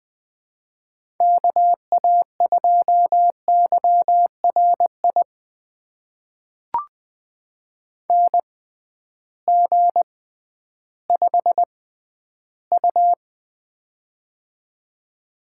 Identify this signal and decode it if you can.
Morse code, heard as a keyed tone, about 20 wpm, spaced as for 12 wpm, 700 Hz. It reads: KA2YRI NG5U
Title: USA Call Signs - 6x Character Spacing - 20wpm